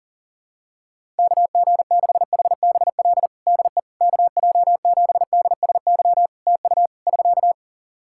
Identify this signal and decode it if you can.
Morse code, heard as a keyed tone, about 40 wpm, 700 Hz. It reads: KG6HBL DE KJ7DSY TU <SK>
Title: K1USN SST QSO Practice 40wpm